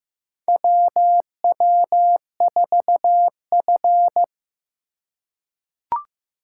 Morse code, heard as a keyed tone, about 15 wpm, 700 Hz. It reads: WW4F
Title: USA Call Signs - 4 Characters - 4x Character Spacing - 15wpm